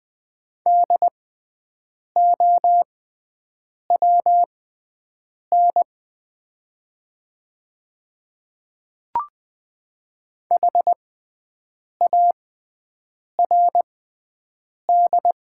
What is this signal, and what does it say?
Morse code, heard as a keyed tone, about 20 wpm, spaced as for 7 wpm, 700 Hz. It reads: DOWN HARD
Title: 6x Character Spacing - Top 300 Words - 20wpm